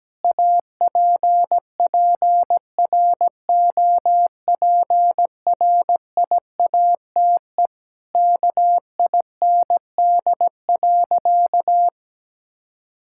Morse code, heard as a keyed tone, about 17 wpm, 700 Hz. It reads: APPROPRIATE KIND.
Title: Sentences from Top 1000 Words - Review Part 5 - 17wpm